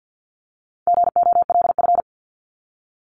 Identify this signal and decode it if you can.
Morse code, heard as a keyed tone, about 50 wpm, 700 Hz. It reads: GOLF
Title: Top 1751-2000 Words 50wpm